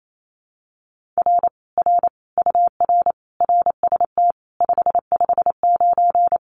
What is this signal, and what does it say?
Morse code, heard as a keyed tone, about 28 wpm, 700 Hz. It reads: R R UR RST 559
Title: Common QSO Phrases Encore 28wpm